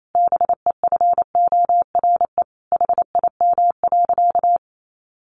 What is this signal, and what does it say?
Morse code, heard as a keyed tone, about 28 wpm, 700 Hz. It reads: BEFORE HIM.
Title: Sentences from Top 1400 Words 28wpm